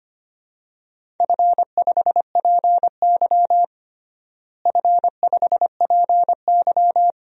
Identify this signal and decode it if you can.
Morse code, heard as a keyed tone, about 25 wpm, 700 Hz. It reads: F5PY F5PY
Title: DXCC-100 Call Signs - 4 Characters - Repeated 2x - 25wpm